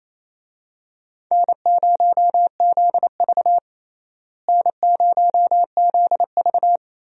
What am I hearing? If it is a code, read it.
Morse code, heard as a keyed tone, about 28 wpm, 700 Hz. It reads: N0ZV N0ZV
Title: USA Call Signs - 4 Characters - Repeated 2x - 28wpm